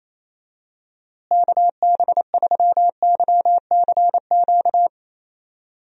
Morse code, heard as a keyed tone, about 28 wpm, 700 Hz. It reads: KB3YCQ
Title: USA Call Signs - 6 Characters - Repeated 3x - 28wpm